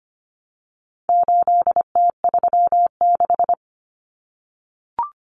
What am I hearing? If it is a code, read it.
Morse code, heard as a keyed tone, about 25 wpm, 700 Hz. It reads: 8T36
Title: Common Cut Numbers - 4-Digit 25wpm